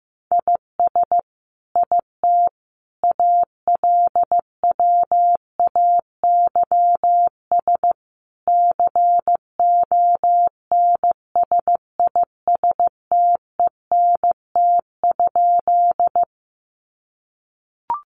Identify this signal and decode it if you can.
Morse code, heard as a keyed tone, about 15 wpm, 700 Hz. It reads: IS IT ALWAYS CONSISTENT?
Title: Sentences from Top 1500 Words 15wpm